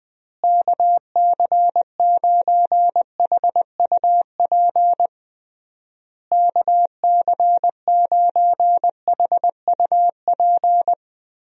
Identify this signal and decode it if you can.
Morse code, heard as a keyed tone, about 21 wpm, 700 Hz. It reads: KC9HUP KC9HUP
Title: USA Call Signs - 6 Characters - Repeated 3x - 20wpm